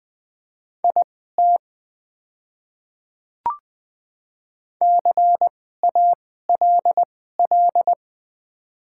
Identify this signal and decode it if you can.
Morse code, heard as a keyed tone, about 20 wpm, spaced as for 14 wpm, 700 Hz. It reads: IT CALL
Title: 2x Character Spacing - Top 200 Words - 20wpm